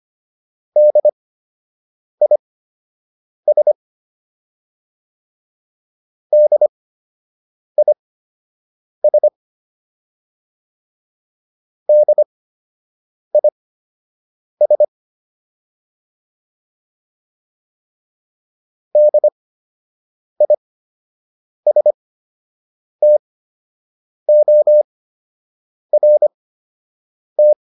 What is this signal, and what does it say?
Morse code, heard as a keyed tone, about 25 wpm, spaced as for 7 wpm, 600 Hz. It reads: DIS DIS DIS DISTORT
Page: fundamental-practice-files